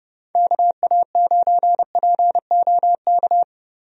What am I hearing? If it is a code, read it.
Morse code, heard as a keyed tone, about 30 wpm, 700 Hz. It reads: KA9POK
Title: USA Call Signs - 6 Characters - Repeated 3x - 30wpm